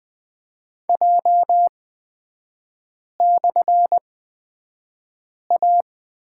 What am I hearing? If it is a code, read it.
Morse code, heard as a keyed tone, about 20 wpm, spaced as for 5 wpm, 690 Hz. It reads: J/A
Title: 229 Lesson 37e - Letter x - Three Characters 20wpm